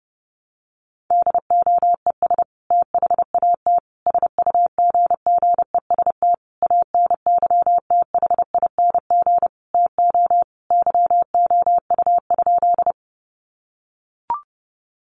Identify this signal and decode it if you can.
Morse code, heard as a keyed tone, about 28 wpm, 700 Hz. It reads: DOES THAT SUGGEST ANYTHING TO YOU?
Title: Sentences from Top 1300 Words 30wpm